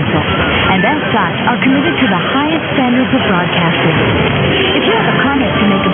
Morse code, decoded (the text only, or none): none